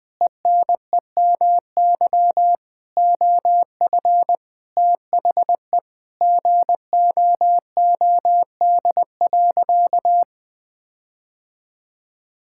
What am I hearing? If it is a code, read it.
Morse code, heard as a keyed tone, about 20 wpm, 700 Hz. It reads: ENEMY OF THE GOOD.
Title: Sentences with 9-Word Phrases - Spoken Beginning - 20wpm